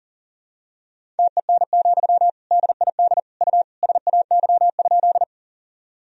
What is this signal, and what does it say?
Morse code, heard as a keyed tone, about 40 wpm, 700 Hz. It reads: TEN, DID U SAY?
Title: Sentences from Top 400 Words and Abbreviations 40wpm